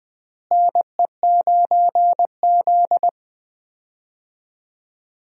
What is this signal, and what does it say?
Morse code, heard as a keyed tone, about 20 wpm, 700 Hz. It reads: NE9Z
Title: USA Call Signs - 4 Characters - Repeated 2x - 20wpm